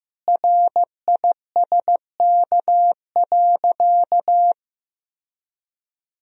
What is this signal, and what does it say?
Morse code, heard as a keyed tone, about 15 wpm, 700 Hz. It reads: RISK.
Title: Sentences from Top 1000 Words - Review Part 4 - 15wpm